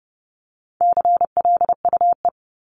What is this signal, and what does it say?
Morse code, heard as a keyed tone, about 30 wpm, 700 Hz. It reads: CLUE
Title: Top 1751-2000 Words 30wpm